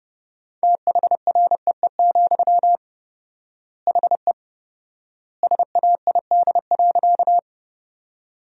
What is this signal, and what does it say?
Morse code, heard as a keyed tone, about 29 wpm, 700 Hz. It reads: THREE, HE SAID.
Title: 4x Word-Spacing - Sentences from Top 200 Words - 30wpm